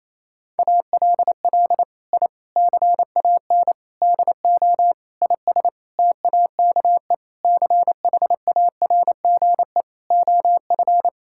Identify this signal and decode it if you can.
Morse code, heard as a keyed tone, about 28 wpm, 700 Hz. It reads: ALL I CAN DO IS TAKE CHARGE OF
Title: Sentences from Top 1000 Words - Review Part 6 - 28wpm